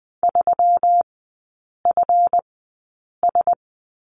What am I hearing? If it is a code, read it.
Morse code, heard as a keyed tone, about 20 wpm, spaced as for 8 wpm, 690 Hz. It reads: 3FS
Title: 145 Lesson 24e - Letter y - Three Characters 20wpm